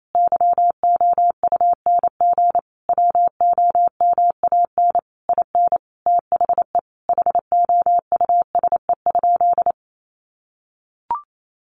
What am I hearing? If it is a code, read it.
Morse code, heard as a keyed tone, about 28 wpm, 700 Hz. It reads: YOUNG WOMAN IN THE HOUSE?
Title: Sentences from Top 1100 Words 28wpm